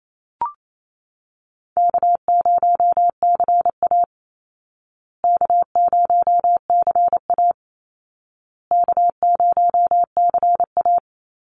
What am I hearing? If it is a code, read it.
Morse code, heard as a keyed tone, about 28 wpm, 700 Hz. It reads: K0CA K0CA K0CA
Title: Call Signs Repeated 2x - 28wpm